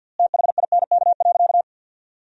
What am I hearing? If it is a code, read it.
Morse code, heard as a keyed tone, about 50 wpm, 700 Hz. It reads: THINK.
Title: 3x Word-Spacing - Sentences from Top 400 Words - 50wpm